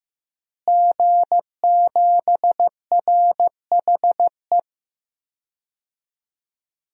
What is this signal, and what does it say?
Morse code, heard as a keyed tone, about 15 wpm, 700 Hz. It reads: G7RHE
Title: DXCC-100 Call Signs - 5 Characters - Repeated 3x - 15wpm